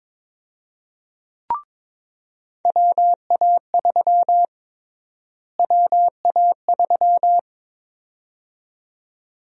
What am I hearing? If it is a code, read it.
Morse code, heard as a keyed tone, about 22 wpm, 700 Hz. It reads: WA3 WA3
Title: USA Call Sign Prefixes - Repeated 2x - 22wpm